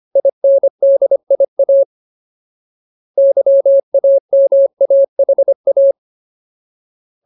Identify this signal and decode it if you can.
Morse code, heard as a keyed tone, about 25 wpm, 550 Hz. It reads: INDIA YAMAHA
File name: Ultimate Word List / 25WPM Ultimate